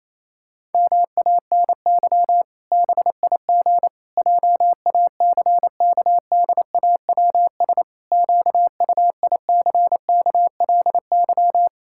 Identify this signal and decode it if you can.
Morse code, heard as a keyed tone, about 28 wpm, 700 Hz. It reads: MANY BIG JACKDAWS QUICKLY
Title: Pangram Challenge v2 - 28wpm